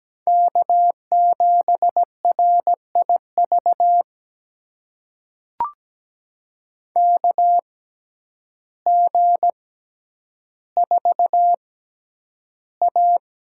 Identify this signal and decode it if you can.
Morse code, heard as a keyed tone, about 17 wpm, spaced as for 9 wpm, 700 Hz. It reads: K7RIV KG4A
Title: USA Call Signs - 6x Character Spacing - 17wpm